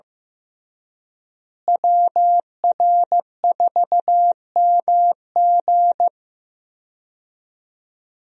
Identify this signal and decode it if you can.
Morse code, heard as a keyed tone, about 15 wpm, 700 Hz. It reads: WR4MG
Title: USA Call Signs - 5 Characters - Repeated 3x - 15wpm